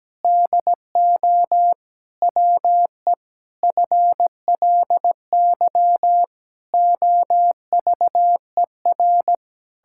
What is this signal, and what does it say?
Morse code, heard as a keyed tone, about 17 wpm, 700 Hz. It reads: DO WE FLY OVER
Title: Pangram Challenge v2 - 17wpm